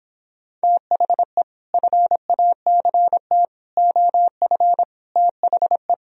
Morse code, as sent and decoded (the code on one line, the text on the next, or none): - .... / ..-. .- -.-. - / --- ..-. / - .... .
THE FACT OF THE